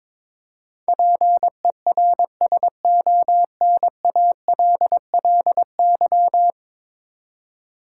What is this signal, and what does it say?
Morse code, heard as a keyed tone, about 22 wpm, 700 Hz. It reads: PERSONALLY
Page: morse-code-ninja-practice-22wpm